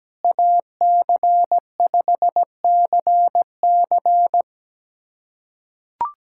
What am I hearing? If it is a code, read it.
Morse code, heard as a keyed tone, about 17 wpm, 700 Hz. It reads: AC5CC
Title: USA Callsign DE Callsign 17wpm